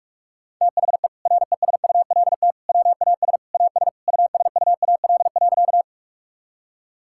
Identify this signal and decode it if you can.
Morse code, heard as a keyed tone, about 46 wpm, 700 Hz. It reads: THE RESULT WAS AS USUAL.
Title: Sentences from Top 1000 Words - Review Part 5 - 45wpm